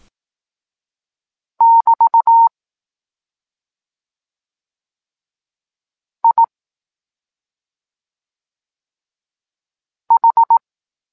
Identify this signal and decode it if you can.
Morse code, heard as a keyed tone, about 18 wpm, 920 Hz. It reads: <BT> I H
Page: Tekst 23.3 – Ciąg dalszy doskonalenia odbioru poznanych liter